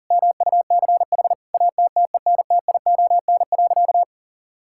Morse code, heard as a keyed tone, about 40 wpm, 700 Hz. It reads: MUCH ATTENTION.
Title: Sentences from Top 1000 Words - Review Part 6 - 40wpm